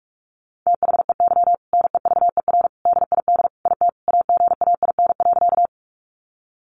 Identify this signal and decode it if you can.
Morse code, heard as a keyed tone, about 44 wpm, 700 Hz. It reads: THEY NEVER DID IT AGAIN.
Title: Sentences from Top 300 Words 45wpm